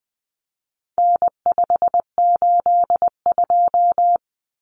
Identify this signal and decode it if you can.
Morse code, heard as a keyed tone, about 20 wpm, 700 Hz. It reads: N582
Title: Common Cut Numbers - 4-Digit 20wpm